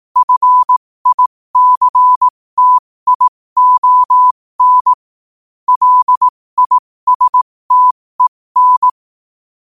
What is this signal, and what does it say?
Morse code, heard as a keyed tone, about 18 wpm, spaced as for 15 wpm, 1000 Hz. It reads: FICTION LISTEN
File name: qotdfiles / 15wpm